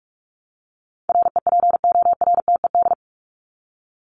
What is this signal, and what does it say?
Morse code, heard as a keyed tone, about 44 wpm, 700 Hz. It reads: REPORTED
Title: ED Suffix - Top 100 Words with Base Words 45wpm